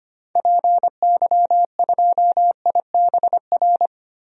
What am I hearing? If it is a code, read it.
Morse code, heard as a keyed tone, about 25 wpm, 700 Hz. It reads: PY2IBR
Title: DXCC-100 Call Signs - 6 Characters - Repeated 2x - 25wpm